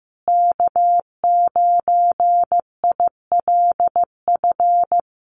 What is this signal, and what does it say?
Morse code, heard as a keyed tone, about 15 wpm, 700 Hz. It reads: K9ILF
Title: USA Call Signs Repeated 3x 15wpm